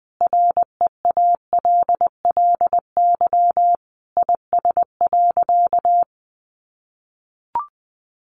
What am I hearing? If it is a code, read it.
Morse code, heard as a keyed tone, about 20 wpm, 700 Hz. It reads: REALLY IS.
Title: Sentences from Top 600 Words 20wpm